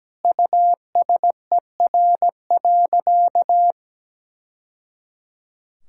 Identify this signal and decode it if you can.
Morse code, heard as a keyed tone, about 17 wpm, 700 Hz. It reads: USER.
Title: Sentences from Top 1000 Words - Review Part 5 - 17wpm